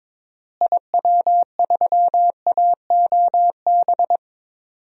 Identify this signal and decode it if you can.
Morse code, heard as a keyed tone, about 21 wpm, 700 Hz. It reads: IW3AOB
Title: DXCC-100 Call Signs - 6 Characters - Repeated 2x - 22wpm